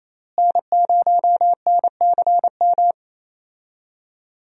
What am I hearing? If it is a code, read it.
Morse code, heard as a keyed tone, about 27 wpm, 700 Hz. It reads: N0NCM
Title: USA Call Signs - 5 Characters - Repeated 3x - 28wpm